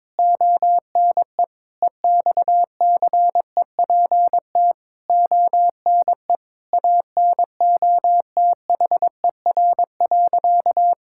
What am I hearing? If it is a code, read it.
Morse code, heard as a keyed tone, about 22 wpm, 700 Hz. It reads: ONE EXCEPT ONE ANOTHER.
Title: Sentences from Top 1600 Words 22wpm